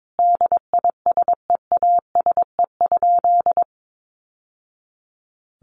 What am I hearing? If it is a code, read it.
Morse code, heard as a keyed tone, about 22 wpm, 700 Hz. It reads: DISEASE?
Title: Sentences from Top 1200 Words 22wpm